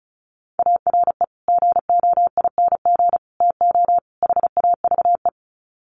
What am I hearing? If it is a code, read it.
Morse code, heard as a keyed tone, about 34 wpm, 700 Hz. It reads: ARE GOING TO HAVE
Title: Sets of 4 Words 35wpm